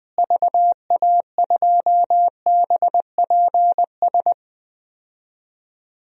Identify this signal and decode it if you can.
Morse code, heard as a keyed tone, about 20 wpm, 700 Hz. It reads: VA2BPS
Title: DXCC-100 Call Signs - 6 Characters - Repeated 2x - 20wpm